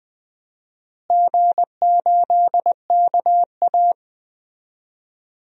Morse code, heard as a keyed tone, about 20 wpm, 700 Hz. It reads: G8KA
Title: DXCC-100 Call Signs - 4 Characters - Repeated 3x - 20wpm